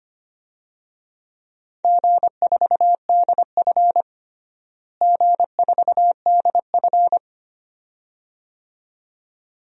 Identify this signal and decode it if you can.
Morse code, heard as a keyed tone, about 25 wpm, 700 Hz. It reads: G4DF G4DF
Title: DXCC-100 Call Signs - 4 Characters - Repeated 2x - 25wpm